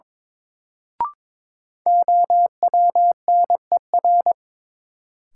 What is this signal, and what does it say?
Morse code, heard as a keyed tone, about 22 wpm, 700 Hz. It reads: OWNER
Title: Sentences from Top 1600 Words - New Word Drill - 22wpm